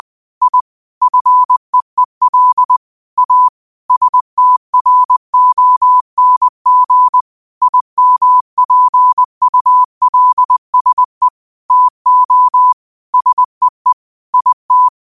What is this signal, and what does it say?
Morse code, heard as a keyed tone, about 20 wpm, 1000 Hz. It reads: I FEEL A STRONG IMPULSE TO SEE IT
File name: qotdfiles / 20wpm